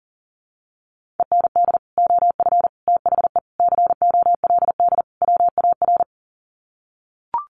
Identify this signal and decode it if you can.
Morse code, heard as a keyed tone, about 40 wpm, 700 Hz. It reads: END OF THE COLD WAR
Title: Sets of 5 Words Encore 40wpm